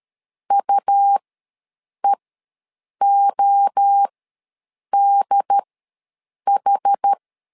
Morse code, heard as a keyed tone, about 13 wpm, spaced as for 7 wpm, 800 Hz. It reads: UEODH
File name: LEZIONI / 3 MWRO / 3 MWRO ws 12wpm 1914